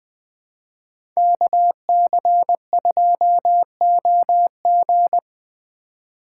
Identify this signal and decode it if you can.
Morse code, heard as a keyed tone, about 20 wpm, 700 Hz. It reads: KC2OG
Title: Speed Racing - USA 2x2 Call Signs 17wpm